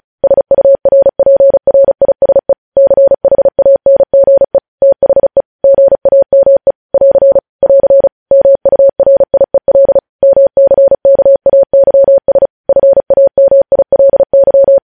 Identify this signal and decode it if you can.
Morse code, heard as a keyed tone, about 36 wpm, 550 Hz. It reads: SURPRISE CHANGE THE GAME <AR> <AR> MURIEL MCKAYS FAMILY